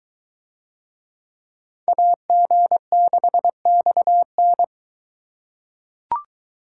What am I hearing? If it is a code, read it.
Morse code, heard as a keyed tone, about 22 wpm, 700 Hz. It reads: AG6XN